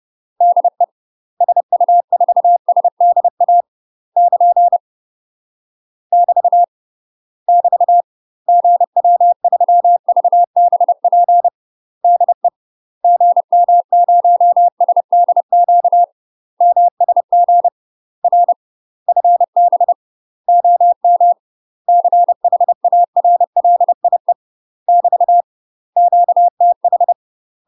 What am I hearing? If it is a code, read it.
Morse code, heard as a keyed tone, about 30 wpm, 700 Hz. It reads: DE SU4SDA <KN> <BT> <BT> GW3VBP DE GM0SDQ MSG R FB OM CHARLIE <BT> QTH
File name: UK MASTER QSOs TRAINING / 30WPM / UK MASTER QSOs 0041